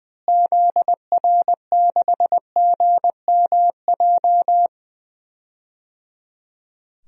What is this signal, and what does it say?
Morse code, heard as a keyed tone, about 21 wpm, 700 Hz. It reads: ZR6GMJ
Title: DXCC-100 Call Signs - 6 Characters - Repeated 3x - 20wpm